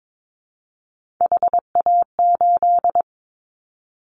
Morse code, heard as a keyed tone, about 22 wpm, 700 Hz. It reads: HA8
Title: Callsign Prefixes for Top 25 DXCC - Repeated 3x - 22wpm